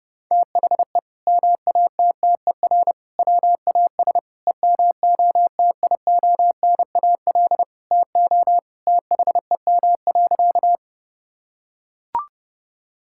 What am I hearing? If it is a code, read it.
Morse code, heard as a keyed tone, about 30 wpm, 700 Hz. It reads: THE MATTER WAS EMOTIONAL TO THEM.
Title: Sentences from Top 1000 Words - Review Part 5 - 30wpm